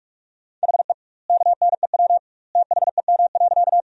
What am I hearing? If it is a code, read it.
Morse code, heard as a keyed tone, about 41 wpm, 700 Hz. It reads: HE KNEW THEM.
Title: 2x Word-Spacing - Sentences from Top 400 Words - 45wpm